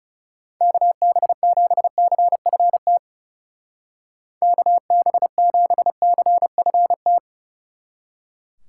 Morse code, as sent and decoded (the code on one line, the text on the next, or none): -.- -... --... -.-. ..-. - / -.- -... --... -.-. ..-. -
KB7CFT KB7CFT